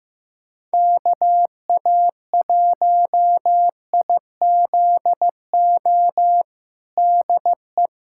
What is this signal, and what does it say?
Morse code, heard as a keyed tone, about 15 wpm, 700 Hz. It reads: KA1IZO DE
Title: USA Callsign DE Callsign - 6 Characters - 15wpm